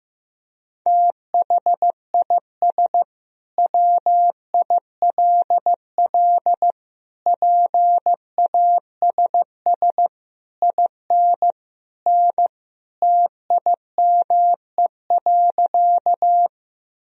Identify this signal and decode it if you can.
Morse code, heard as a keyed tone, about 16 wpm, 700 Hz. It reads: THIS WILL PASS IN N TIME.
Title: Sentences from Top 400 Words and Abbreviations 15wpm